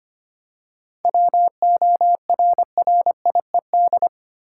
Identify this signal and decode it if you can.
Morse code, heard as a keyed tone, about 25 wpm, 700 Hz. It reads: WORRIED
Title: Seven Letter Words - 25wpm